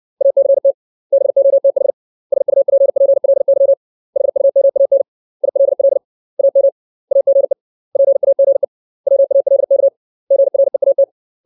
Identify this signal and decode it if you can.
Morse code, heard as a keyed tone, about 60 wpm, 550 Hz. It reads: ACT BOTH SUPPLY HUMAN ILL AM AGE PAGE WALK GREAT
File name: 632 words / 632 Words 60WPM